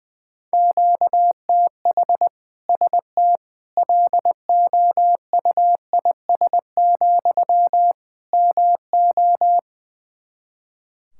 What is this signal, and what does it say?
Morse code, heard as a keyed tone, about 21 wpm, 700 Hz. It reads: QTH ST LOUIS, MO
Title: Practice QSO - Exchange 1 - Beginning - 20wpm